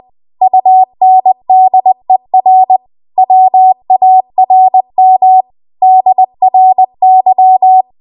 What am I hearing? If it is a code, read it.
Morse code, heard as a keyed tone, about 20 wpm, 750 Hz. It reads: UNDER WARM DRY
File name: Archive / 20 WPM / 20WPM